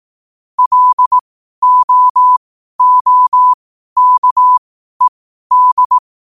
Morse code, heard as a keyed tone, about 18 wpm, spaced as for 12 wpm, 1000 Hz. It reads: LOOKED
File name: qotdfiles / 12wpm